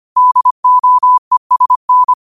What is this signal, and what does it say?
Morse code, heard as a keyed tone, about 25 wpm, 1000 Hz. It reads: DOESN
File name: qotdfiles / 25wpm